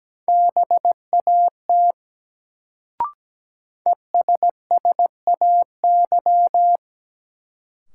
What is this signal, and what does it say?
Morse code, heard as a keyed tone, about 17 wpm, 700 Hz. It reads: BAT ESSAY
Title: Sentences from Top 1500 Words - New Word Drill - 17wpm